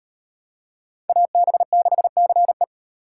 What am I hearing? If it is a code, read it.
Morse code, heard as a keyed tone, about 35 wpm, 700 Hz. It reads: AB6CE